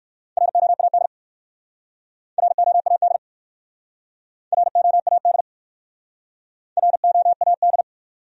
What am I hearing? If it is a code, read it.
Morse code, heard as a keyed tone, about 63 wpm, 690 Hz. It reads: ROAD ROAD ROAD ROAD